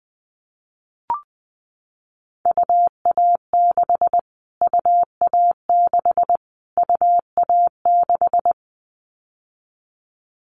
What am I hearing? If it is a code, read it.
Morse code, heard as a keyed tone, about 20 wpm, 700 Hz. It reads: UA6 UA6 UA6
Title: Callsign Prefixes for Top 25 DXCC - Repeated 3x - 20wpm